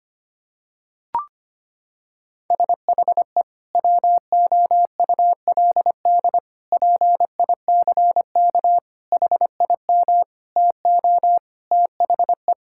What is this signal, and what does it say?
Morse code, heard as a keyed tone, about 25 wpm, 700 Hz. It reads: SHE WOULD PICK HIM TO THE
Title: Sentences from Top 1600 Words 25wpm